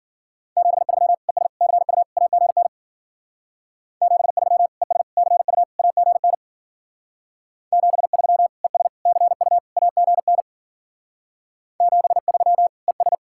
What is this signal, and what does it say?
Morse code, heard as a keyed tone, about 48 wpm, 700 Hz. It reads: 73 ES CU AGN 73 ES CU AGN 73 ES CU AGN 73 ES
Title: Speed-Racing Common QSO Phrases Encore - 40wpm